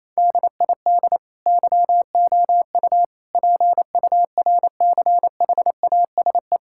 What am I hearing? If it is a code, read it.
Morse code, heard as a keyed tone, about 28 wpm, 700 Hz. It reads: DID YOU PURCHASE
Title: Sentences from Top 1000 Words - Review Part 6 - 28wpm